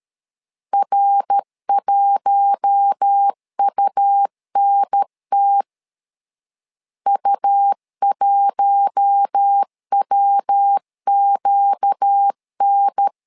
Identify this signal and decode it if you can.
Morse code, heard as a keyed tone, about 13 wpm, 800 Hz. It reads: R1UNT U1WQN
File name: LEZIONI / 4 Q1 12wpm 2028